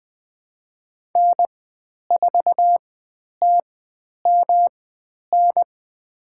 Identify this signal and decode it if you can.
Morse code, heard as a keyed tone, about 20 wpm, spaced as for 10 wpm, 700 Hz. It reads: N4TMN
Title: 109 Lesson 18g - Letter m - Call Signs 20wpm